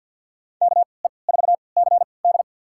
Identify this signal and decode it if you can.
Morse code, heard as a keyed tone, about 51 wpm, spaced as for 28 wpm, 700 Hz. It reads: KE4CD
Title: USA Call Signs - 5 Characters - 3x Character Spacing - 50wpm